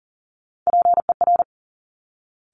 Morse code, heard as a keyed tone, about 42 wpm, 690 Hz. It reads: PER
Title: Top Three Letter Combinations in Words 40wpm